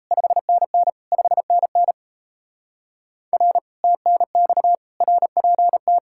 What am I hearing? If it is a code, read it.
Morse code, heard as a keyed tone, about 38 wpm, 700 Hz. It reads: 5NN 5NN R TNX RPT